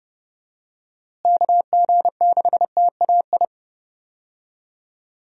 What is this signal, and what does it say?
Morse code, heard as a keyed tone, about 31 wpm, 700 Hz. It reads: KG6TAI
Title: USA Call Signs - 6 Characters - Repeated 2x - 30wpm